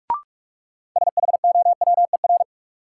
Morse code, heard as a keyed tone, about 47 wpm, 700 Hz. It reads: SHOWER
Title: Sentences from Top 1600 Words - New Word Drill Alt - 45wpm